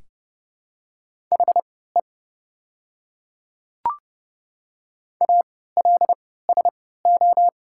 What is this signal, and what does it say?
Morse code, heard as a keyed tone, about 30 wpm, spaced as for 17 wpm, 700 Hz. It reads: HE ALSO
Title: 3x Character Spacing - Top 200 Words - 30wpm